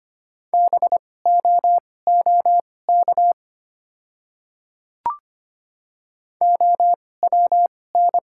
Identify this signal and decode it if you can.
Morse code, heard as a keyed tone, about 25 wpm, spaced as for 18 wpm, 700 Hz. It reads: BOOK OWN
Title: 2x Character Spacing - Top 300 Words - 25wpm